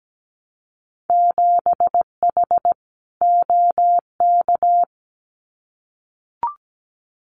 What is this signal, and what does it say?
Morse code, heard as a keyed tone, about 17 wpm, 700 Hz. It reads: 7H OK
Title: Winter Field Day Exchanges Encore 17wpm